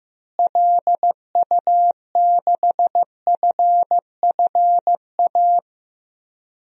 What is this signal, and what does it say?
Morse code, heard as a keyed tone, about 15 wpm, 700 Hz. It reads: LU6FFA
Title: DXCC-100 Call Signs - 6 Characters - Repeated 3x - 15wpm